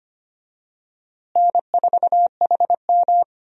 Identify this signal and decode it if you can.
Morse code, heard as a keyed tone, about 25 wpm, 700 Hz. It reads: N4HM
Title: USA Call Signs - 4 Characters - Repeated 3x - 25wpm